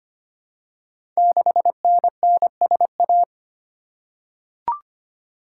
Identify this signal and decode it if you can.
Morse code, heard as a keyed tone, about 25 wpm, 700 Hz. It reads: 6NNSA
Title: Code Groups - 5 Letter:Numbers - 25wpm